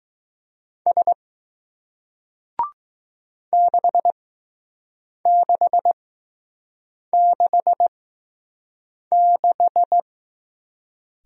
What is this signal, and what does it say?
Morse code, heard as a keyed tone, about 20 wpm, spaced as for 6 wpm, 700 Hz. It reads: S 6666